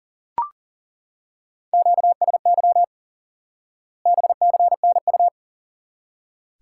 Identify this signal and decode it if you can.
Morse code, heard as a keyed tone, about 37 wpm, 700 Hz. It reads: QSY BCNU
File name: course / instant-qso-element-course-3 / IQE Course 3 - Lesson 89 - YAGI - Two Elements - 40wpm